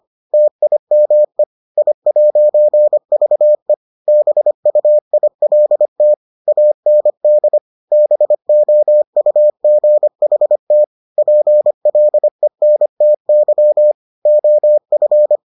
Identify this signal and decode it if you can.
Morse code, heard as a keyed tone, about 25 wpm, 600 Hz. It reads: TIME I'VE BUILT AND BOUGHT PLENTY OF